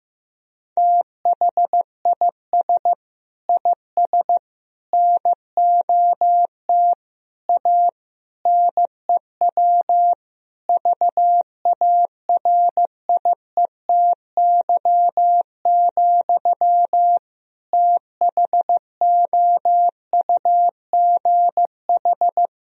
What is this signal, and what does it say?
Morse code, heard as a keyed tone, about 15 wpm, 700 Hz. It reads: THIS IS NOT A NEW VARIETY, THOUGH
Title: Sentences from Top 1000 Words - Review Part 4 - 15wpm